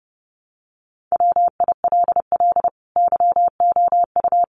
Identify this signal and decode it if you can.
Morse code, heard as a keyed tone, about 30 wpm, 700 Hz. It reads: WILL YOU
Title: Sentences from Top 200 Words 30wpm